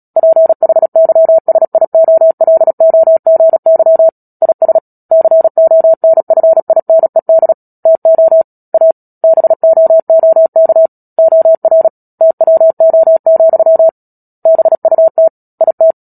Morse code, heard as a keyed tone, about 37 wpm, 650 Hz. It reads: PHYSIOLOGY IS CONFINED TO A BOOK OR TWO, BUT IT
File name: Book Two / 10-Chapter 10 - Epilogue - 36 WPM